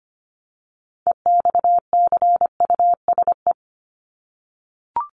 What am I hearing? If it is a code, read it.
Morse code, heard as a keyed tone, about 25 wpm, 700 Hz. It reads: EXCUSE
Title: Top 501-750 Words 25wpm